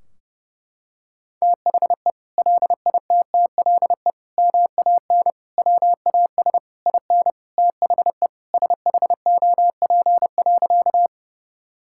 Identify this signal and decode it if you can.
Morse code, heard as a keyed tone, about 30 wpm, 700 Hz. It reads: THE LITTLE MAN WAS IN THE SHOP.
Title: Sentences from Top 1600 Words 30wpm